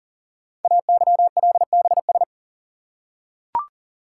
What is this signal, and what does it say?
Morse code, heard as a keyed tone, about 40 wpm, 690 Hz. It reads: AYLBS
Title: Code Groups - 5 Letter:Numbers - 40wpm